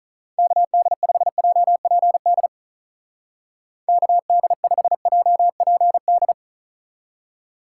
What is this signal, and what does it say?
Morse code, heard as a keyed tone, about 42 wpm, 700 Hz. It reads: KD5JPD KD5JPD
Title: Speed Racing - USA 2x3 Call Signs 35wpm